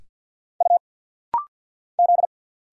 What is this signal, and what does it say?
Morse code, heard as a keyed tone, about 50 wpm, 700 Hz. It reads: U Z
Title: Single Letters - Rapid-Fire 50wpm